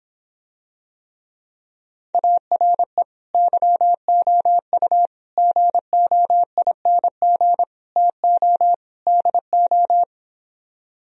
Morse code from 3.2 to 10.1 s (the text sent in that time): YOU GOING TO DO